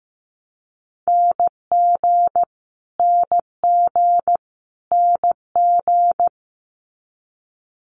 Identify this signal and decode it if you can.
Morse code, heard as a keyed tone, about 15 wpm, 700 Hz. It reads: NG NG NG
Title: Multi-letter Phonemes Repeated 3x 15wpm